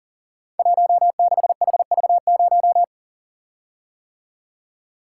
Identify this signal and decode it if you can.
Morse code, heard as a keyed tone, about 41 wpm, 700 Hz. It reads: 16HV0
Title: Code Groups - 5 Letter:Numbers - 40wpm